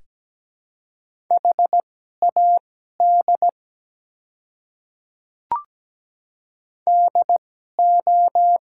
Morse code, heard as a keyed tone, about 17 wpm, spaced as for 12 wpm, 700 Hz. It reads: HAD DO
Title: 2x Character Spacing - Top 100 Words - 17wpm